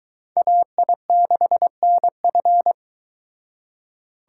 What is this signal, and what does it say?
Morse code, heard as a keyed tone, about 22 wpm, 700 Hz. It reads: AI6NF